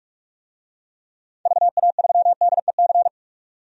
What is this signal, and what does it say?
Morse code, heard as a keyed tone, about 41 wpm, 700 Hz. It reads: VA3DEC
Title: DXCC-100 Call Signs - 6 Characters - 45wpm